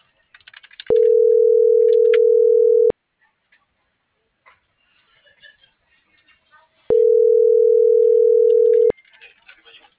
Ringback tone (North America), 440 + 480 Hz.
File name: CENTRAL MULTIMODAL / out rodolfo.fuentes - .71156